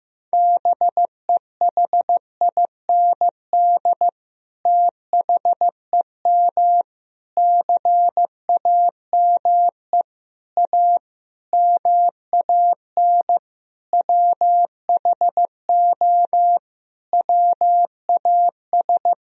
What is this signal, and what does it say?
Morse code, heard as a keyed tone, about 15 wpm, 700 Hz. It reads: BEHIND THEM CAME A MAN WHO WAS
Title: Sentences from Top 1000 Words - Review Part 4 - 15wpm